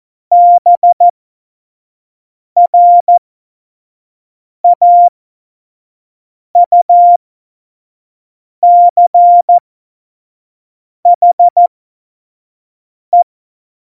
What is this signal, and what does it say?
Morse code, heard as a keyed tone, about 14 wpm, spaced as for 5 wpm, 700 Hz. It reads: BRAUCHE
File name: Morsekurs / Witze / Witz 14. November 2017, 25 F70